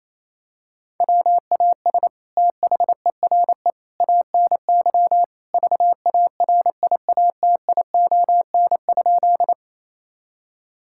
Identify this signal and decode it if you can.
Morse code, heard as a keyed tone, about 28 wpm, 700 Hz. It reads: WAS THERE ANY VARIATION?
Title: Sentences from Top 1600 Words 28wpm